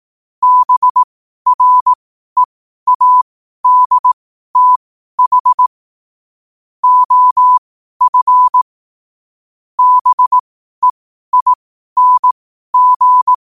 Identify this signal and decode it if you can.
Morse code, heard as a keyed tone, about 18 wpm, spaced as for 12 wpm, 1000 Hz. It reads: BREADTH OF BEING
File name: qotdfiles / 12wpm